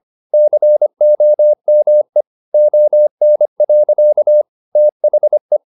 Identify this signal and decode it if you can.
Morse code, heard as a keyed tone, about 25 wpm, 600 Hz. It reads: COME ON. THE
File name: the old man and the sea part 1 25wpm